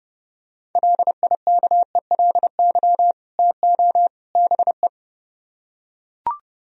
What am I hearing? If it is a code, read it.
Morse code, heard as a keyed tone, about 30 wpm, 700 Hz. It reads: LIKELY TO BE